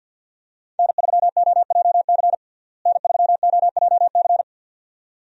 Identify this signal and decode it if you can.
Morse code, heard as a keyed tone, about 53 wpm, 700 Hz. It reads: N3OJC N3OJC
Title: USA Call Signs - 5 Characters - Repeated 2x - 50wpm